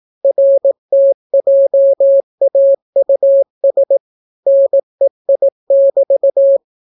Morse code, heard as a keyed tone, about 18 wpm, 550 Hz. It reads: RTJAUS NEI<BT>